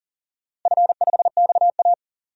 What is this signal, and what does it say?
Morse code, heard as a keyed tone, about 40 wpm, 700 Hz. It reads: F5XA
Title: DXCC-100 Call Signs - 4 Characters - Repeated 3x - 40wpm